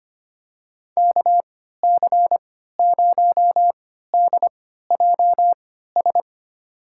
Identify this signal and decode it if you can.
Morse code, heard as a keyed tone, about 25 wpm, spaced as for 14 wpm, 700 Hz. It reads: KC0DJS
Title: USA Call Signs - 6 Characters - 3x Character Spacing - 25wpm